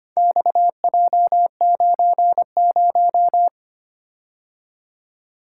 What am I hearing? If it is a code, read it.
Morse code, heard as a keyed tone, about 25 wpm, 700 Hz. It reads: XJ90
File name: practice / 25wpm / Four Letter-Number Combinations - 25wpm